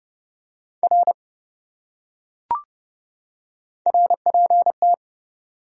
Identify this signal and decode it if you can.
Morse code, heard as a keyed tone, about 27 wpm, 700 Hz. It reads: R RPT